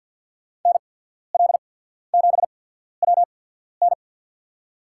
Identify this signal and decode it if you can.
Morse code, heard as a keyed tone, about 53 wpm, spaced as for 14 wpm, 700 Hz. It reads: NL7WN
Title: USA Call Signs - 5 Characters - 8x Character Spacing - 50wpm